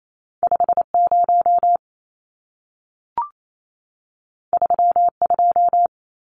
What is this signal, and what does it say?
Morse code, heard as a keyed tone, about 28 wpm, 700 Hz. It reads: 50 32
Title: Numbers - 2-Digit 28wpm